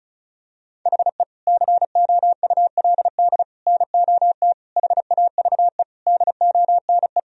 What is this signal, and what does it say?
Morse code, heard as a keyed tone, about 34 wpm, 700 Hz. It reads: HE COULD NOT HAVE DONE
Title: Sentences from Top 1000 Words - Review Part 2 - 35wpm